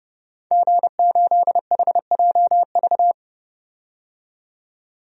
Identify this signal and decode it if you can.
Morse code, heard as a keyed tone, about 32 wpm, 700 Hz. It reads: G8HJV
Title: DXCC-100 Call Signs - 5 Characters - 30wpm